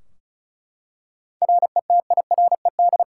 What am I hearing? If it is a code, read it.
Morse code, heard as a keyed tone, about 36 wpm, 700 Hz. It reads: RETIRED
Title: Practice QSO - Exchange 3 - More Info - 35wpm